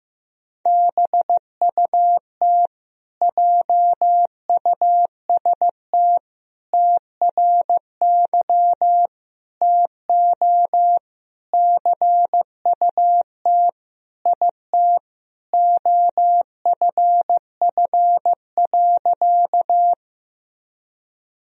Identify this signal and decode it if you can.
Morse code, heard as a keyed tone, about 15 wpm, 700 Hz. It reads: BUT JUST TRY TO CUT IT OFF.
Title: Sentences from Top 1000 Words - Review Part 1 - 15wpm